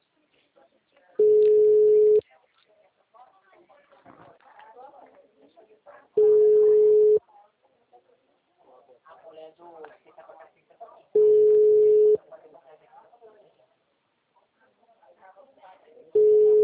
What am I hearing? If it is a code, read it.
Ringback tone (Europe, 425 Hz), with speech.